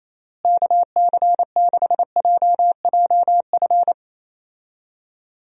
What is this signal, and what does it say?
Morse code, heard as a keyed tone, about 28 wpm, 700 Hz. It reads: KC6JJF
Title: USA Call Signs - 6 Characters - Repeated 2x - 28wpm